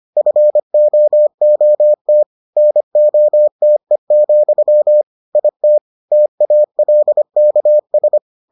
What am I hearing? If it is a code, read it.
Morse code, heard as a keyed tone, about 25 wpm, 600 Hz. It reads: FOOT NOTE, IT TALKS